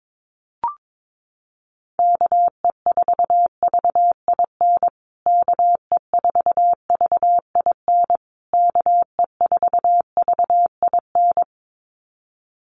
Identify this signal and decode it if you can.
Morse code, heard as a keyed tone, about 22 wpm, 700 Hz. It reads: KE4VIN KE4VIN KE4VIN
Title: USA Call Signs Repeated 3x 22wpm